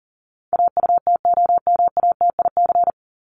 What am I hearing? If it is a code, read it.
Morse code, heard as a keyed tone, about 40 wpm, 700 Hz. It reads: AUTOMATIC
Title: Top 1751-2000 Words 40wpm